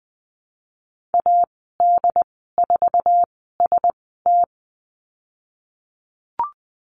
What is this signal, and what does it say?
Morse code, heard as a keyed tone, about 20 wpm, spaced as for 15 wpm, 700 Hz. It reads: AD4ST
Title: 088 Lesson 15g - Number 5 - Call Signs 20wpm